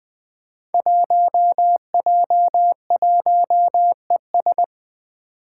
Morse code, heard as a keyed tone, about 20 wpm, 700 Hz. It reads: 1J1ES
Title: Five Challenging Letter-Number Combinations 20wpm